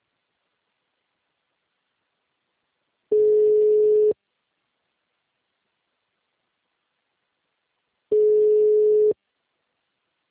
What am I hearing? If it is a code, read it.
Ringback tone (Europe, 425 Hz).
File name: out- -2016- - - .65